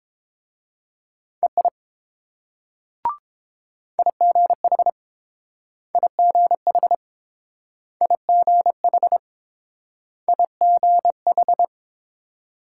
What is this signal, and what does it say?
Morse code, heard as a keyed tone, about 28 wpm, 700 Hz. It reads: EI IGH IGH IGH IGH